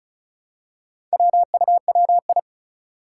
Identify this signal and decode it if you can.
Morse code, heard as a keyed tone, about 36 wpm, 700 Hz. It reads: WUWI
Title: Four Challenging Letter-Number Combinations 35wpm